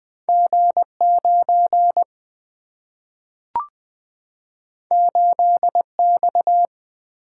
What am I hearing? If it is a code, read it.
Morse code, heard as a keyed tone, about 20 wpm, 700 Hz. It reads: G 9 8 X
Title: Single-Letter Single-Number 20wpm